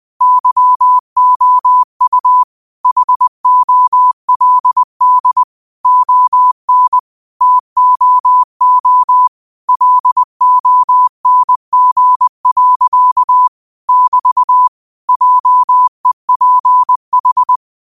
Morse code, heard as a keyed tone, about 20 wpm, 1000 Hz. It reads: YOU HOLD ON TOO LONG. <BT> JEPH